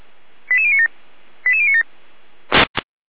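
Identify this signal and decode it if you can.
Five-tone selective call (ZVEI), heard as quick runs of five tones, tones of about 70 ms. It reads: 88097 88097
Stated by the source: ZVEI five-tone call 89361 received by SDR, archived by fire brigade Feuerwehr Orlamünde.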